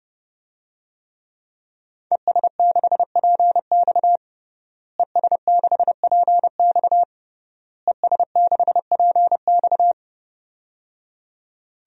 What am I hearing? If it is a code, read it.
Morse code, heard as a keyed tone, about 30 wpm, 700 Hz. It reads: ES6PX ES6PX ES6PX
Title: DXCC-100 Call Signs - 5 Characters - Repeated 3x - 30wpm